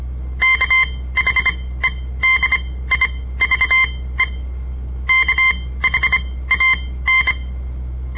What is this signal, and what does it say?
Morse code, heard as a keyed tone, about 24 wpm, spaced as for 16 wpm, 2100 Hz. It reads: KHEDIVE KHAN
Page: On the air Morse code practice from the Southern Montana Amatuer Radio Association; all code, no read back